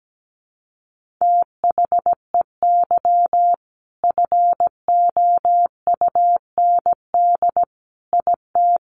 Morse code, heard as a keyed tone, about 17 wpm, 700 Hz. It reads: THEY FOUND IT
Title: Sentences from Top 1400 Words 17wpm